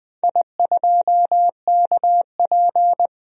Morse code, heard as a keyed tone, about 20 wpm, 700 Hz. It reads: I2KP
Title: DXCC-100 Call Signs - 4 Characters - Repeated 3x - 20wpm